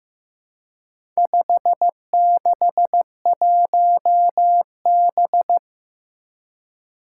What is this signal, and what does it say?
Morse code, heard as a keyed tone, about 15 wpm, 700 Hz. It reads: 561B
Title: Four Challenging Letter-Number Combinations 15wpm